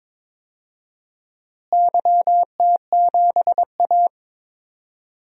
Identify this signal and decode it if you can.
Morse code, heard as a keyed tone, about 22 wpm, 700 Hz. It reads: YT7A
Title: DXCC-100 Call Signs - 4 Characters - Repeated 3x - 22wpm